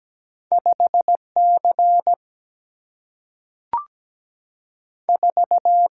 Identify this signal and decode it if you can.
Morse code, heard as a keyed tone, about 17 wpm, 700 Hz. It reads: 5 C 4
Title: Single-Letter Single-Number 17wpm